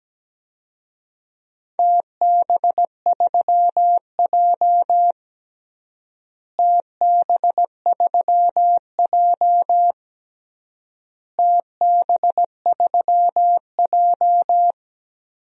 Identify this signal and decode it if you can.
Morse code, heard as a keyed tone, about 17 wpm, 700 Hz. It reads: TB3J TB3J TB3J
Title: DXCC-100 Call Signs - 4 Characters - Repeated 3x - 17wpm